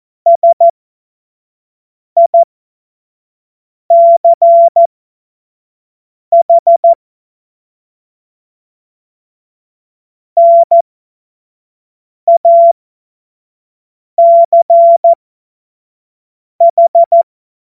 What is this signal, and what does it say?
Morse code, heard as a keyed tone, about 14 wpm, spaced as for 5 wpm, 680 Hz. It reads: SICH NACH
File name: Morsekurs / Witze / Witz 04. Juli 2023 25 F70